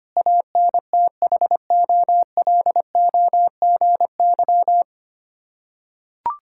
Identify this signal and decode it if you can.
Morse code, heard as a keyed tone, about 25 wpm, 700 Hz. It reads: ANTHOLOGY
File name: practice / 25wpm / Difficult Words with Hints - 25wpm